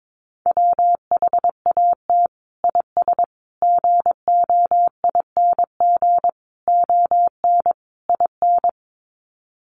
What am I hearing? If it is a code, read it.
Morse code, heard as a keyed tone, about 22 wpm, 700 Hz. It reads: WHAT IS GOING ON IN
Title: Sets of 5 Words Encore2 22wpm